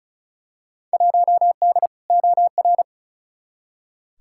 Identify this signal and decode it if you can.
Morse code, heard as a keyed tone, about 36 wpm, 700 Hz. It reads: 1D OR
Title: Field Day Exchanges - 35wpm